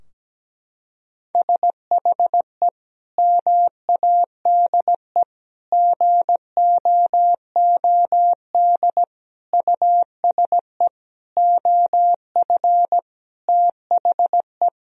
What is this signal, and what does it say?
Morse code, heard as a keyed tone, about 17 wpm, 700 Hz. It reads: SHE MADE GOOD USE OF THE
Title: Sentences from Top 1000 Words - Review Part 4 - 17wpm